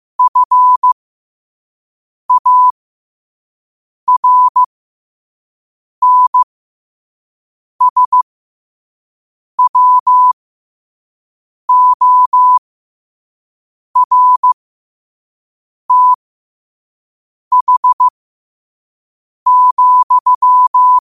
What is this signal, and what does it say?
Morse code, heard as a keyed tone, about 15 wpm, spaced as for 5 wpm, 1000 Hz. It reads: FARNSWORTH,
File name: 5wpm